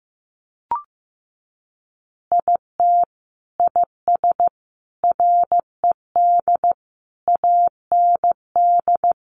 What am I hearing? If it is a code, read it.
Morse code, heard as a keyed tone, about 15 wpm, 700 Hz. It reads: IT IS RED AND
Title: Additional Easy Phrases 15wpm